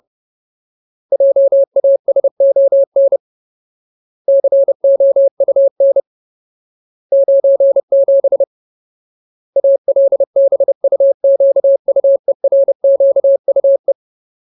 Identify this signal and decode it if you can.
Morse code, heard as a keyed tone, about 30 wpm, 550 Hz. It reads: JASON COUN 97 ALBUQUERQUE
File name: Ultimate Word List / 30WPM Ultimate